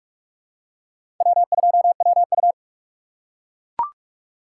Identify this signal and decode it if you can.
Morse code, heard as a keyed tone, about 42 wpm, 700 Hz. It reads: W2WU
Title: USA Call Signs - 4 Characters - 4x Character Spacing - 45wpm